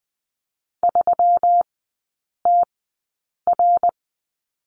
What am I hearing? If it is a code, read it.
Morse code, heard as a keyed tone, about 20 wpm, spaced as for 8 wpm, 700 Hz. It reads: 3TR
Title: 184 Lesson 30e - Letter b - Three Characters 20wpm